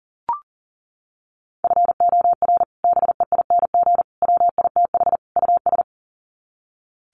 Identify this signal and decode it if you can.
Morse code, heard as a keyed tone, about 40 wpm, 700 Hz. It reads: FOR BEING WITH US
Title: Sets of 4 Words 40wpm